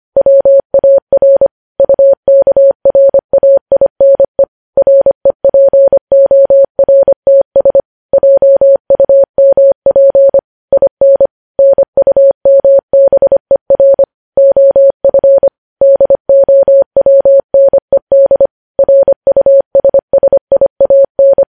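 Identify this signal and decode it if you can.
Morse code, heard as a keyed tone, about 25 wpm, 550 Hz. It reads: WAR UKRAINE REPORTS JUMP IN NUMBER OF DOWNED RUSSIAN
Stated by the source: Morse code transmission of news headlines at 25 wpm.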